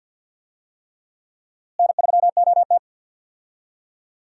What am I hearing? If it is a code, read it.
Morse code, heard as a keyed tone, about 49 wpm, 700 Hz. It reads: N3OT
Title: USA Call Signs - 4 Characters - Repeated 3x - 50wpm